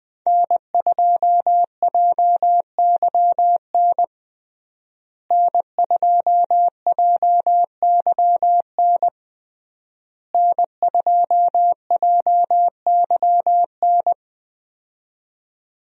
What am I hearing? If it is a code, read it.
Morse code, heard as a keyed tone, about 20 wpm, 700 Hz. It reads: N2JYN N2JYN N2JYN
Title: USA Call Signs - 5 Characters - Repeated 3x - 20wpm